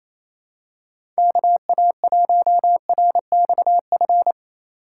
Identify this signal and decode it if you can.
Morse code, heard as a keyed tone, about 28 wpm, 700 Hz. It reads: KA1RXF
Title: USA Call Signs - 4x Character Spacing - 28wpm